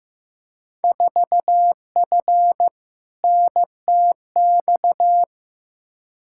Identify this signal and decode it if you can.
Morse code, heard as a keyed tone, about 15 wpm, 700 Hz. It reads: 4F NTX
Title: Field Day Exchanges Encore - 15wpm